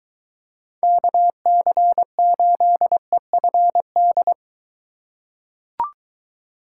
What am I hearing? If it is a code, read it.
Morse code, heard as a keyed tone, about 21 wpm, 700 Hz. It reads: KC8EFD